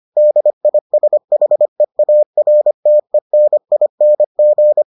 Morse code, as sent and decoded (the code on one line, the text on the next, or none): -.. .. ... .... .- .-. - . -. .. -. --.
DISHEARTENING